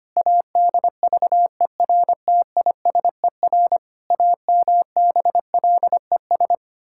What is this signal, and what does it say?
Morse code, heard as a keyed tone, about 25 wpm, 700 Hz. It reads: ADVERTISER AMBLES
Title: Pangram Challenge v2 - 25wpm